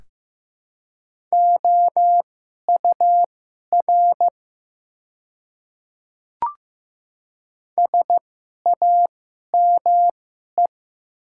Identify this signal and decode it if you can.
Morse code, heard as a keyed tone, about 15 wpm, spaced as for 11 wpm, 700 Hz. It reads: OUR SAME
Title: 2x Character Spacing - Top 200 Words - 15wpm